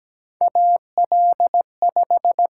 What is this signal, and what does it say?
Morse code, heard as a keyed tone, about 17 wpm, 700 Hz. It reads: AL5
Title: USA Call Signs - 6x Character Spacing - 17wpm